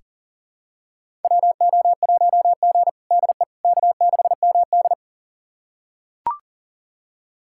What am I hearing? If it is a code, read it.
Morse code, heard as a keyed tone, about 40 wpm, 700 Hz. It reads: WO1G DE K6MD
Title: USA Callsign DE Callsign - 4 Characters - 40wpm